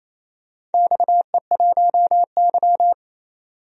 Morse code, heard as a keyed tone, about 28 wpm, 700 Hz. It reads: XE1Y
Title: DXCC-100 Call Signs - 4 Characters - 28wpm